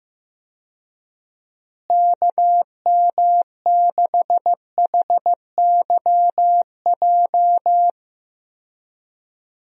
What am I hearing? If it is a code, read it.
Morse code, heard as a keyed tone, about 15 wpm, 700 Hz. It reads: KM6HYJ
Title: USA Call Signs - 6 Characters - Repeated 3x - 15wpm